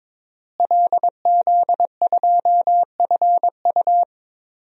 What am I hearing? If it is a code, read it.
Morse code, heard as a keyed tone, about 22 wpm, 700 Hz. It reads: LZ2FU
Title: DXCC-100 Call Signs - 5 Characters - Repeated 2x - 22wpm